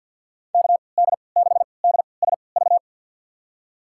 Morse code, heard as a keyed tone, about 51 wpm, spaced as for 28 wpm, 700 Hz. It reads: KD6DSV
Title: USA Call Signs - 3x Character Spacing - 50wpm